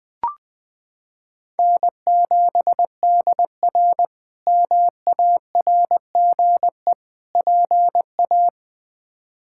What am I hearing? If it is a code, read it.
Morse code, heard as a keyed tone, about 20 wpm, 700 Hz. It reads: N7DR MARGE PA
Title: K1USN SST QSO Practice - 20wpm